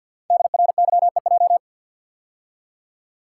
Morse code, heard as a keyed tone, about 50 wpm, 690 Hz. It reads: DRYEJ
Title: Code Groups - 5 Letters - 50wpm